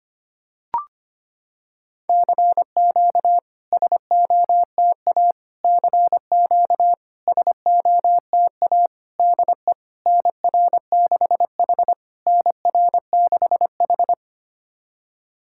Morse code, heard as a keyed tone, about 25 wpm, 700 Hz. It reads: CQ SOTA CQ SOTA DE NR6H NR6H
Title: SOTA QSO Practice 25wpm